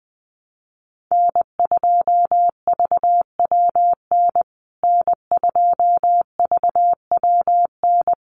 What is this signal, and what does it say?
Morse code, heard as a keyed tone, about 20 wpm, 700 Hz. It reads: N2VWN N2VWN
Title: USA Call Signs Repeated 3x 20wpm